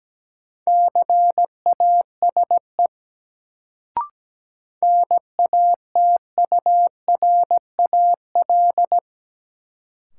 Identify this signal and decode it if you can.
Morse code, heard as a keyed tone, about 17 wpm, 700 Hz. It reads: CASE NATURAL
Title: Sentences from Top 700 Words - New Word Drill - 17wpm